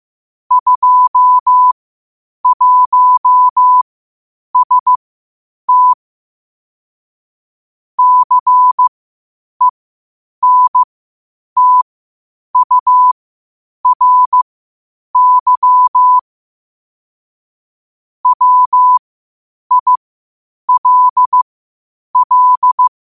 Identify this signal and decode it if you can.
Morse code, heard as a keyed tone, about 15 wpm, spaced as for 8 wpm, 1000 Hz. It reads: 21ST CENTURY WILL
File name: 8wpm